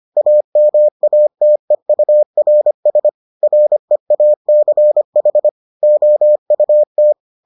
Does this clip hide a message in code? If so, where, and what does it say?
Yes: Morse code from 0.2 to 7.1 s, heard as a keyed tone, about 25 wpm, 600 Hz. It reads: AMATEURS REACH OUT